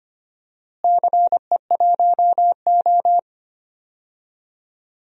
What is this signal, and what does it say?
Morse code, heard as a keyed tone, about 25 wpm, 700 Hz. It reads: CE1O
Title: DXCC-100 Call Signs - 4 Characters - Repeated 3x - 25wpm